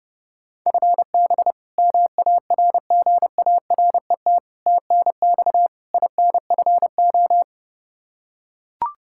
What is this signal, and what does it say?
Morse code, heard as a keyed tone, about 29 wpm, 700 Hz. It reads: FB MARGARET TNX INFO
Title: Practice QSO - Exchange 3 - More Info - 30wpm